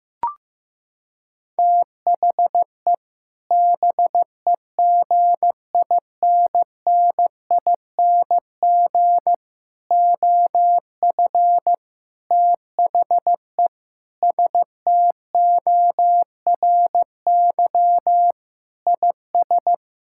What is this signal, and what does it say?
Morse code, heard as a keyed tone, about 15 wpm, 700 Hz. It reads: THE BEGINNING OF THE STORY IS
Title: Sentences from Top 1000 Words - Review Part 6 - 15wpm